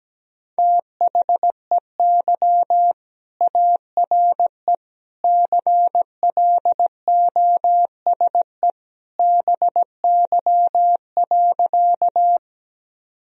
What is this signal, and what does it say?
Morse code, heard as a keyed tone, about 17 wpm, 700 Hz. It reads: THEY ARE CLOSE BY.
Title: Sentences from Top 1000 Words - Review Part 1 - 17wpm